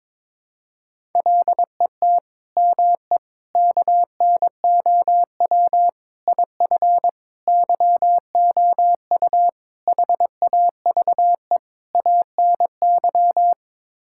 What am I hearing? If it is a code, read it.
Morse code, heard as a keyed tone, about 22 wpm, 700 Hz. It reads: LET ME KNOW IF YOU HAVE ANY
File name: practice / 22wpm / Sets of 7 Words - 22wpm